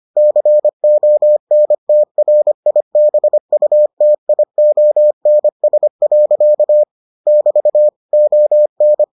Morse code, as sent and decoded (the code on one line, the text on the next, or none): -.-. --- -. - .-. .. -... ..- - .. --- -. ... .-.-.- / -...- / --- -.
CONTRIBUTIONS. <BT> ON